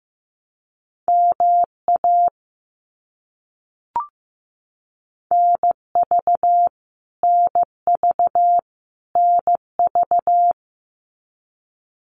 Morse code, heard as a keyed tone, about 15 wpm, 700 Hz. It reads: MA NV NV NV
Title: US States Repeated 3x 15wpm